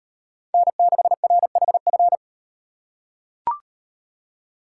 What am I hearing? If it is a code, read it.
Morse code, heard as a keyed tone, about 38 wpm, 700 Hz. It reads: N6RHF